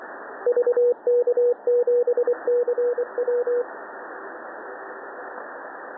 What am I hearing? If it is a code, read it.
Morse code, heard as a keyed tone, about 23 wpm, 490 Hz, with radio noise. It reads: VK7CW
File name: TN5E wkg VK7CW 20CW